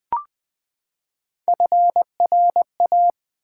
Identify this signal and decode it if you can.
Morse code, heard as a keyed tone, about 20 wpm, 700 Hz. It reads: FRA
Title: Build A Word - 3 Letters At A Time - 9-Letter Words - 20wpm